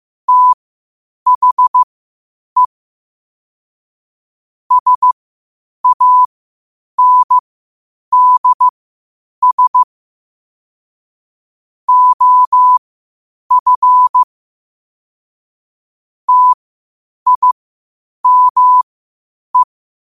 Morse code, heard as a keyed tone, about 15 wpm, spaced as for 8 wpm, 1000 Hz. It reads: THE SANDS OF TIME
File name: qotdfiles / 8wpm